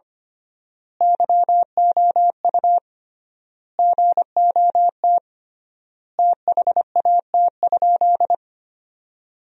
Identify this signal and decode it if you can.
Morse code, heard as a keyed tone, about 25 wpm, 700 Hz. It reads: YOU GOT THAT?
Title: 3x Word-Spacing - Sentences from Top 300 Words - 25wpm